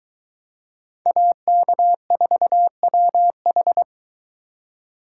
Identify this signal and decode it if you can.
Morse code, heard as a keyed tone, about 22 wpm, 700 Hz. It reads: AK4WH